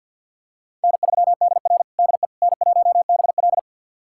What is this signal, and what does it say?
Morse code, heard as a keyed tone, about 52 wpm, 700 Hz. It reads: N3DR DE N1BL
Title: USA Callsign DE Callsign - 4 Characters - 50wpm